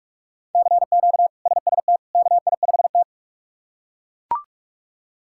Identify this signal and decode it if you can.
Morse code, heard as a keyed tone, about 41 wpm, 700 Hz. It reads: CQ SST KI5T
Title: K1USN SST QSO Practice - 45wpm